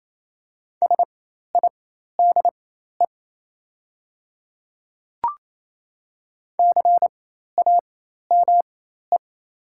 Morse code, heard as a keyed tone, about 28 wpm, spaced as for 13 wpm, 700 Hz. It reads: SIDE CAME
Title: 4x Character Spacing - Top 200 Words - 28wpm